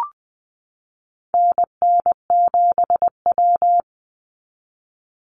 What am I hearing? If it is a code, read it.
Morse code, heard as a keyed tone, about 20 wpm, 700 Hz. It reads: NN7W
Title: Call Signs Repeated 3x - 20wpm